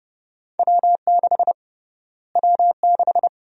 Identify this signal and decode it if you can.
Morse code, heard as a keyed tone, about 30 wpm, 700 Hz. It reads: W6 W6
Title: USA Call Sign Prefixes - Repeated 2x - 30wpm